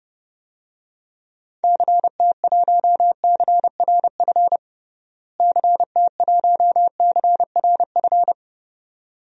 Morse code, heard as a keyed tone, about 30 wpm, 700 Hz. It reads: CT1CRF CT1CRF
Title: DXCC-100 Call Signs - 6 Characters - Repeated 2x - 30wpm